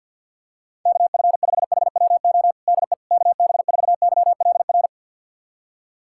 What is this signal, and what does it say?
Morse code, heard as a keyed tone, about 50 wpm, 700 Hz. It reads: KF5HWO DE KB4YLR
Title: USA Callsign DE Callsign - 6 Characters - 50wpm